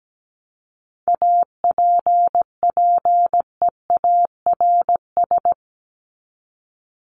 Morse code, heard as a keyed tone, about 17 wpm, 700 Hz. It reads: APPEARS
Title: ES:S Suffix - Top 100 Words with Base Words 17wpm